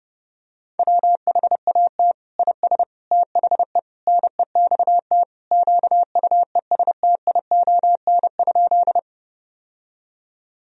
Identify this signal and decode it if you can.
Morse code, heard as a keyed tone, about 32 wpm, 700 Hz. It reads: WHAT IS THE NEXT QUESTION?
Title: Sentences from Top 1000 Words - Review Part 1 - 30wpm